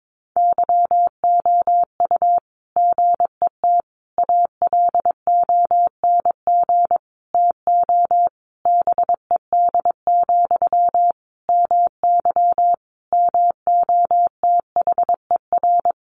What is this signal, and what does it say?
Morse code, heard as a keyed tone, about 22 wpm, 700 Hz. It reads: YOU GET ALONG TO BED, MY MOTHER
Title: Sentences from Top 700 Words 22wpm